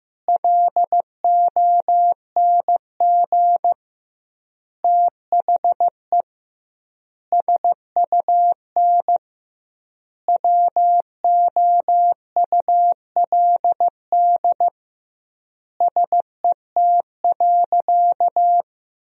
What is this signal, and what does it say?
Morse code, heard as a keyed tone, about 15 wpm, 700 Hz. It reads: LONG THE SUN WOULD SET.
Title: 2x Word-Spacing - Sentences from Top 400 Words - 15wpm